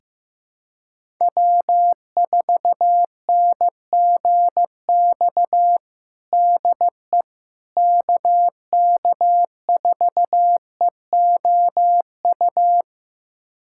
Morse code, heard as a keyed tone, about 15 wpm, 700 Hz. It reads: W4NGX DE KK4EOU
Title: USA Callsign DE Callsign 15wpm